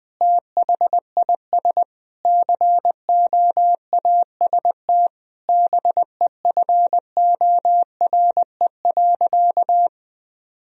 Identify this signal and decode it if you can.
Morse code, heard as a keyed tone, about 20 wpm, 700 Hz. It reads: THIS COAST BEFORE.
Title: Sentences from Top 1000 Words - Review Part 4 - 20wpm